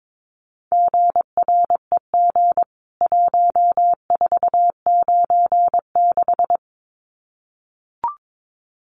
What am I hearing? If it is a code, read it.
Morse code, heard as a keyed tone, about 22 wpm, 700 Hz. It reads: GREG 1496
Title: CWT Exchange 22wpm